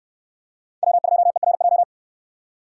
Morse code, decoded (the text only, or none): W2ERJ